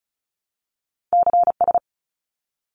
Morse code, heard as a keyed tone, about 36 wpm, 700 Hz. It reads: CS
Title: Abbreviations 35wpm